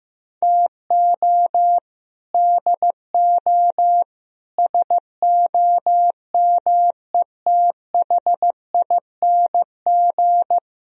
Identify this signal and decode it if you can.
Morse code, heard as a keyed tone, about 15 wpm, 700 Hz. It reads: TO DO SOMETHING
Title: Speed-Racing - Sets of 5 Words 15wpm